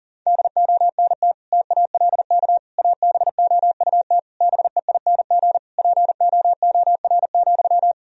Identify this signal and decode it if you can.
Morse code, heard as a keyed tone, about 40 wpm, 700 Hz. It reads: DONT TALK ABOUT BEING POOR,
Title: Sentences from Top 1000 Words - Review Part 4 - 40wpm